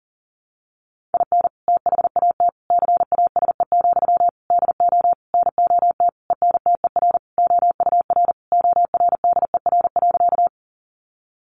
Morse code, heard as a keyed tone, about 40 wpm, 700 Hz. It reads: IN THAT CASE, DO NOT ENTER OUR ORDER.
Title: Sentences from Top 1300 Words 40wpm